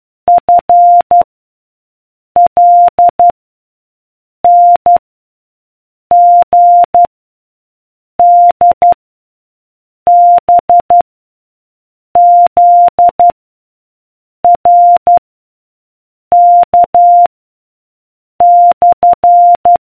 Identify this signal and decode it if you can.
Morse code, heard as a keyed tone, about 12 wpm, 700 Hz. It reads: F L N G D B Z R K /